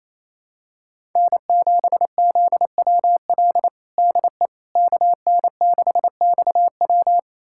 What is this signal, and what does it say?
Morse code, heard as a keyed tone, about 28 wpm, 700 Hz. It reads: N7ZWL DE KN6XW
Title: USA Callsign DE Callsign - 5 Characters - 28wpm